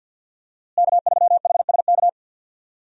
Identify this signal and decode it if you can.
Morse code, heard as a keyed tone, about 52 wpm, 700 Hz. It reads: K3HSK
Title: USA Call Signs - 5 Characters - Repeated 2x - 50wpm